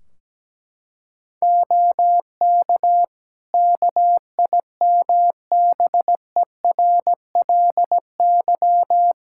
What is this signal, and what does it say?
Morse code, heard as a keyed tone, about 17 wpm, 700 Hz. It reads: OK KIMBERLY
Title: Practice QSO - Exchange 4 - Ending - 17wpm